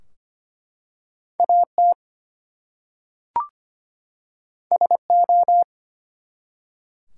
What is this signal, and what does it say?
Morse code, heard as a keyed tone, about 25 wpm, 700 Hz. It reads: AT SO
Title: Two Letter Words - 25wpm